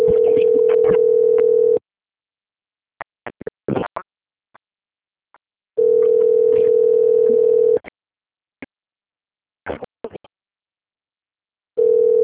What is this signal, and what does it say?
Ringback tone (North America), 440 + 480 Hz.
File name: out- - - - - .638